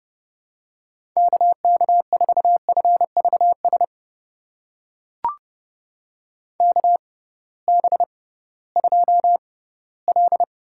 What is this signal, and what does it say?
Morse code, heard as a keyed tone, about 30 wpm, spaced as for 18 wpm, 700 Hz. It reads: KK4FVS KB2L
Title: USA Call Signs - 6 Characters - 6x Character Spacing - 30wpm